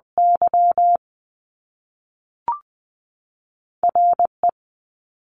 Morse code, heard as a keyed tone, about 20 wpm, 700 Hz. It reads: Y R E
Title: Top Two Letter Combinations in Words 20wpm